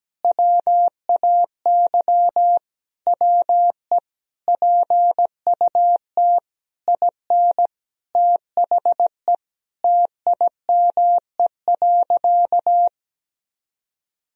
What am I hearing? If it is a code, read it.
Morse code, heard as a keyed tone, about 17 wpm, 700 Hz. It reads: WAY WE PUT IN THE TIME.
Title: Sentences from Top 200 Words and Abbreviations 17wpm